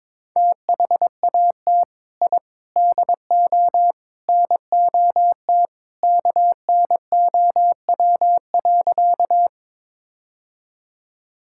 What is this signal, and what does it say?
Morse code, heard as a keyed tone, about 22 wpm, 700 Hz. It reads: THAT I DO NOT KNOW.
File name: practice / 22wpm / Sentences with 4-Word Phrases - 22wpm